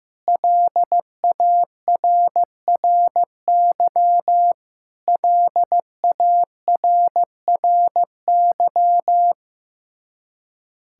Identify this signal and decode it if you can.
Morse code, heard as a keyed tone, about 15 wpm, 700 Hz. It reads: LARRY LARRY
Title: Practice QSO - Exchange 1 - Beginning - 15wpm